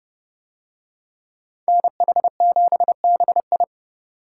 Morse code, heard as a keyed tone, about 30 wpm, 700 Hz. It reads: NH7BI